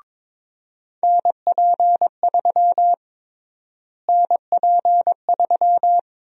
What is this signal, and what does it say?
Morse code, heard as a keyed tone, about 22 wpm, 700 Hz. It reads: NP3 NP3
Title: USA Call Sign Prefixes - Repeated 2x - 22wpm